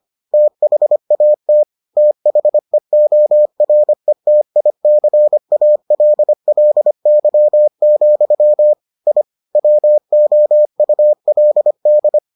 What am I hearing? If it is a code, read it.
Morse code, heard as a keyed tone, about 25 wpm, 600 Hz. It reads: THAT THEORETICALLY, I WOULD